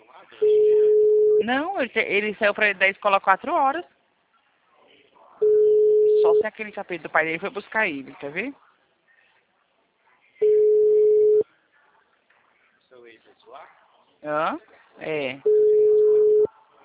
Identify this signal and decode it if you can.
Ringback tone (Europe, 425 Hz), with speech.